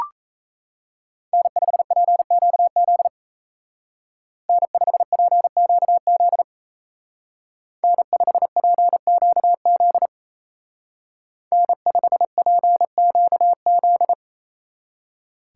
Morse code, heard as a keyed tone, about 32 wpm, 700 Hz. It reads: N5PQZ N5PQZ N5PQZ N5PQZ
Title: Speed Racing - USA 1x3 Call Signs 28wpm